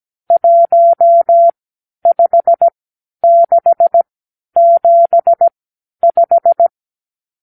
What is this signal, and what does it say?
Morse code, heard as a keyed tone, about 17 wpm, spaced as for 10 wpm, 680 Hz. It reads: 15675
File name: k7qo / Lesson 52 (8)